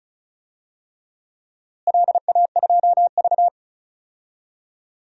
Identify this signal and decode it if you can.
Morse code, heard as a keyed tone, about 34 wpm, 700 Hz. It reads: LA2V
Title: DXCC-100 Call Signs - 4 Characters - Repeated 3x - 35wpm